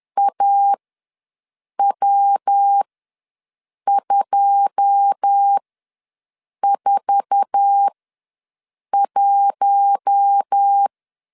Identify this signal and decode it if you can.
Morse code, heard as a keyed tone, about 10 wpm, spaced as for 6 wpm, 800 Hz. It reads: AW241
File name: LEZIONI / 9 286 ws 10wpm 4032